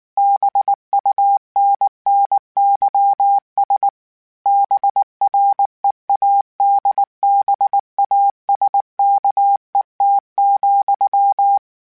Morse code, heard as a keyed tone, about 19 wpm, 800 Hz. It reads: BUNNYS BREADBASKET,
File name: Misc / Misc-mp3s / BIG @18